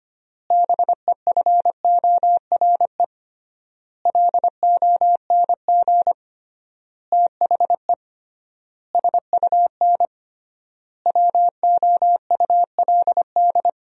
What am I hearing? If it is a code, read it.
Morse code, heard as a keyed tone, about 25 wpm, 700 Hz. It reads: BEFORE LONG THE SUN WOULD
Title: 3x Word-Spacing - Sentences from Top 400 Words - 25wpm